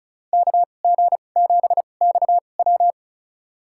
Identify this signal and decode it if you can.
Morse code, heard as a keyed tone, about 36 wpm, spaced as for 25 wpm, 700 Hz. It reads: KG7XW
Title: USA Call Signs - 5 Characters - 2x Character Spacing - 35wpm